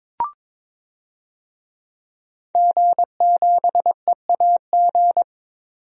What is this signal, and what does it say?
Morse code, heard as a keyed tone, about 22 wpm, 700 Hz. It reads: G7EAG
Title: DXCC-100 Call Signs - 5 Characters - 22wpm